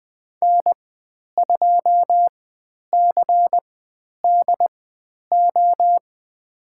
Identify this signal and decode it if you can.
Morse code, heard as a keyed tone, about 20 wpm, spaced as for 10 wpm, 700 Hz. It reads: N2CDO
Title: 095 Lesson 16g - Letter c - Call Signs 20wpm